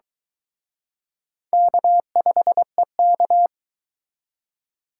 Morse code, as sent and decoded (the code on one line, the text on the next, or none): -.- ..... -.-
K5EK